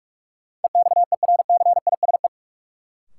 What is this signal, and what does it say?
Morse code, heard as a keyed tone, about 42 wpm, 700 Hz. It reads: EXERCISE
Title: Sentences from Top 1200 Words - New Word Drill Alt - 45wpm